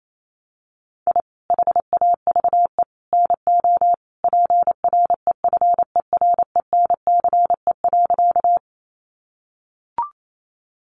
Morse code, heard as a keyed tone, about 28 wpm, 700 Hz. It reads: I HAVE NO PREFERENCE.
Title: Sentences from Top 1500 Words 28wpm